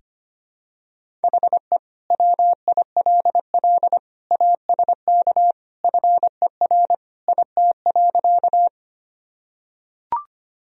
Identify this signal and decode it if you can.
Morse code, heard as a keyed tone, about 25 wpm, 700 Hz. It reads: HE WILL ASK FER IT.
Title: Sentences from Top 200 Words and Abbreviations 25wpm